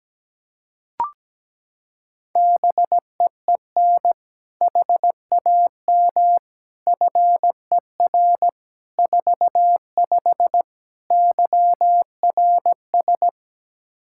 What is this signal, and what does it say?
Morse code, heard as a keyed tone, about 17 wpm, 700 Hz. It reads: BEEN HAM FER 45 YRS
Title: Practice QSO - Exchange 3 - More Info - 17wpm